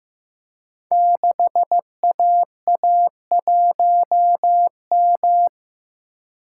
Morse code, heard as a keyed tone, about 15 wpm, 700 Hz. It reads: 6AA1M
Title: Code Groups - 5 Letter:Numbers - 15wpm